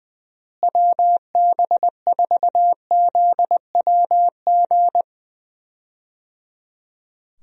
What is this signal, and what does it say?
Morse code, heard as a keyed tone, about 21 wpm, 700 Hz. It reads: WB4ZWG
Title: USA Call Signs - 6 Characters - Repeated 3x - 20wpm